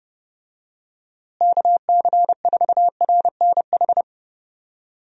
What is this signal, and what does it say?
Morse code, heard as a keyed tone, about 32 wpm, 700 Hz. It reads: KC4RNH
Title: USA Call Signs - 6 Characters - Repeated 3x - 30wpm